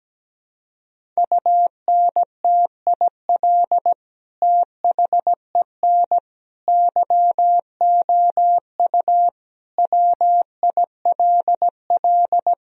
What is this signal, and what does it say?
Morse code, heard as a keyed tone, about 17 wpm, 700 Hz. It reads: UNTIL THEN YOU WILL
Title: Sentences from Top 1000 Words - Review Part 1 - 17wpm